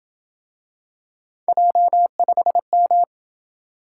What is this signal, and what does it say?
Morse code, heard as a keyed tone, about 27 wpm, 700 Hz. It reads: J5M